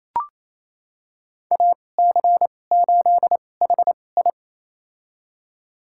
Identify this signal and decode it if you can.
Morse code, heard as a keyed tone, about 28 wpm, spaced as for 20 wpm, 700 Hz. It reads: AC8HI
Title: USA Call Signs - 5 Characters - 2x Character Spacing - 28wpm